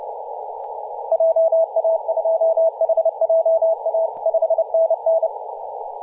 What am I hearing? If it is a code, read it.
Morse code, heard as a keyed tone, about 29 wpm, 650 Hz, with radio noise. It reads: JA2HJA 5NN
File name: posts / 2017-oct-04-0559z-vk9xi-21-032mhz-cw